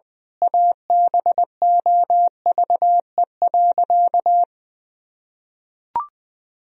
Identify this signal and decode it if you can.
Morse code, heard as a keyed tone, about 20 wpm, 700 Hz. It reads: ABOVE.
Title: Sentences from Top 1000 Words - Review Part 2 - 20wpm